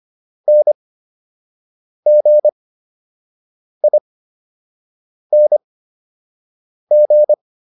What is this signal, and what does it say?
Morse code, heard as a keyed tone, about 25 wpm, spaced as for 6 wpm, 600 Hz. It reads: NGING